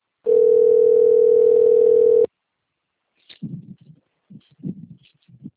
Ringback tone (North America), 440 + 480 Hz.